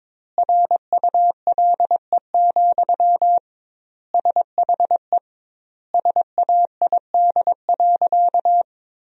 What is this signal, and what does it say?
Morse code, heard as a keyed tone, about 22 wpm, 700 Hz. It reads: RULE, SHE SAID.
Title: 2x Word-Spacing - Sentences from Top 400 Words - 22wpm